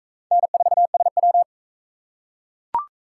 Morse code, heard as a keyed tone, about 38 wpm, 700 Hz. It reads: N4SW